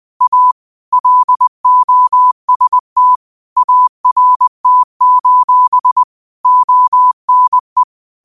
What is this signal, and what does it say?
Morse code, heard as a keyed tone, about 20 wpm, 1000 Hz. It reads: A LOST ART: ONE
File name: qotdfiles / 20wpm